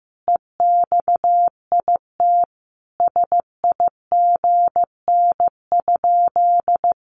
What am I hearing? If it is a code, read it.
Morse code, heard as a keyed tone, about 15 wpm, 700 Hz. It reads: EXIT SIGN?
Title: Sentences from Top 1100 Words 15wpm